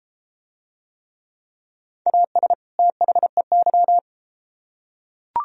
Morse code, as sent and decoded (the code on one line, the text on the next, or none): .- ... / - .... -.--
AS THEY